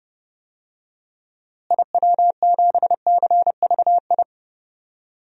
Morse code, heard as a keyed tone, about 31 wpm, 700 Hz. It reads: IW7CVI
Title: DXCC-100 Call Signs - 6 Characters - Repeated 3x - 30wpm